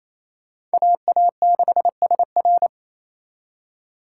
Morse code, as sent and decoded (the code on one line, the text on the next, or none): .- .- -.... ... .-.
AA6SR